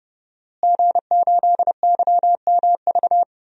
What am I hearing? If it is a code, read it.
Morse code, heard as a keyed tone, about 30 wpm, 700 Hz. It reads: G8YMV
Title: DXCC-100 Call Signs - 5 Characters - Repeated 2x - 30wpm